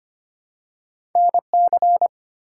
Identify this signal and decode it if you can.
Morse code, heard as a keyed tone, about 25 wpm, 700 Hz. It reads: NC
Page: morse-code-ninja-practice-25wpm